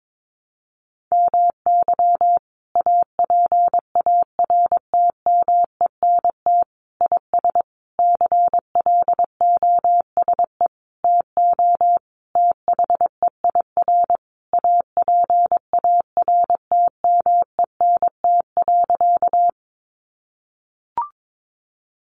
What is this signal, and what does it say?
Morse code, heard as a keyed tone, about 22 wpm, 700 Hz. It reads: MY APARTMENT IS CLOSE TO THEIR APARTMENT.
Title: Sentences from Top 900 Words 22wpm